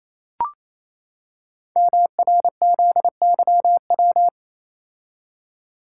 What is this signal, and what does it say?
Morse code, heard as a keyed tone, about 29 wpm, 700 Hz. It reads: MRZYW
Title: Code Groups - 5 Letters - 28wpm